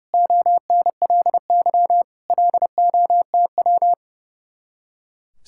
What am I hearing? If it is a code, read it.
Morse code, heard as a keyed tone, about 36 wpm, 700 Hz. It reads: ONLY LOTW
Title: Speed-Racing Common QSO Phrases - 30wpm